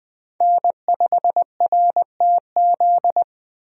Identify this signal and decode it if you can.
Morse code, heard as a keyed tone, about 20 wpm, 700 Hz. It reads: N5RTZ
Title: USA Call Signs - 5 Characters - 20wpm